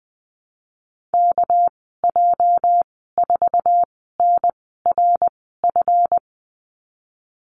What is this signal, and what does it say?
Morse code, heard as a keyed tone, about 19 wpm, spaced as for 14 wpm, 700 Hz. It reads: KJ4NRF
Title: 219 Lesson 35g - Number 0 - Call Signs 20wpm